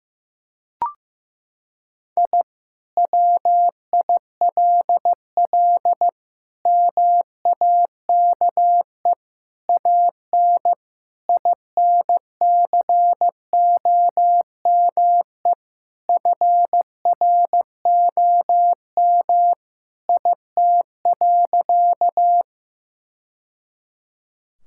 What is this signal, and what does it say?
Morse code, heard as a keyed tone, about 15 wpm, 700 Hz. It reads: I WILL MAKE AN INCOME FROM IT.
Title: Sentences from Top 1000 Words - Review Part 5 - 15wpm